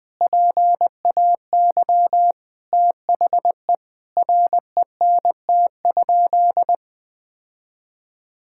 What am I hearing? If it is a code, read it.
Morse code, heard as a keyed tone, about 20 wpm, 700 Hz. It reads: PAY THE RENT?
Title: Sentences from Top 1000 Words - Review Part 6 - 20wpm